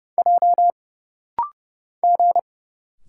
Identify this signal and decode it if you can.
Morse code, heard as a keyed tone, about 30 wpm, 700 Hz. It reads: J G
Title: Single Letters - Rapid-Fire 30wpm